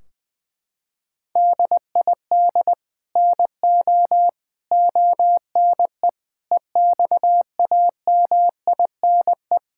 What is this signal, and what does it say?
Morse code, heard as a keyed tone, about 20 wpm, 700 Hz. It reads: DID NO ONE EXAMINE
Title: Sentences from Top 1600 Words 20wpm